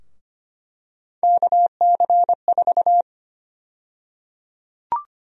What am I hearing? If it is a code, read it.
Morse code, heard as a keyed tone, about 25 wpm, 700 Hz. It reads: KC4
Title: USA Call Sign Prefixes - Repeated 2x - 25wpm